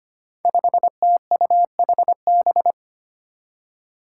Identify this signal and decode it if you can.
Morse code, heard as a keyed tone, about 25 wpm, 700 Hz. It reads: 5TUHB
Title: Five Challenging Letter-Number Combinations 25wpm